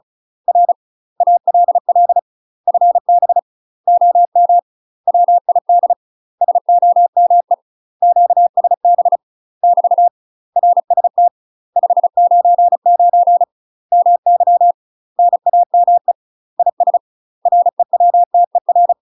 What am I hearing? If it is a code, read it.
Morse code, heard as a keyed tone, about 35 wpm, 700 Hz. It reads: R ALL FB OM WID SOME QSB <BT> RST 599 MY NAME IS REWTER
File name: UK MASTER QSOs TRAINING / 35WPM / UK MASTER QSOs 0035